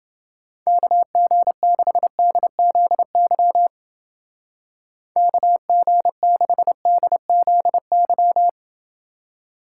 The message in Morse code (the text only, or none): KG6DZY KG6DZY